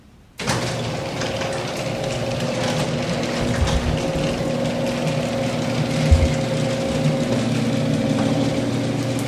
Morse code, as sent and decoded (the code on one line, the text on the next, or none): none